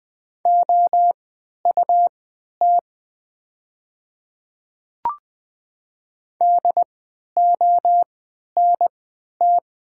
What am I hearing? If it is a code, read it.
Morse code, heard as a keyed tone, about 20 wpm, spaced as for 11 wpm, 700 Hz. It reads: OUT DONT
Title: 3x Character Spacing - Top 300 Words - 20wpm